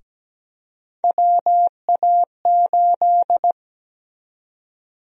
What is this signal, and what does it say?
Morse code, heard as a keyed tone, about 17 wpm, 700 Hz. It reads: WA8
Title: USA Call Sign Prefixes - Repeated 2x - 17wpm